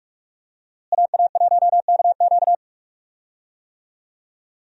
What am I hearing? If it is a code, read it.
Morse code, heard as a keyed tone, about 42 wpm, 700 Hz. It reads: AA1KQ
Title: USA Call Signs - 5 Characters - Repeated 3x - 45wpm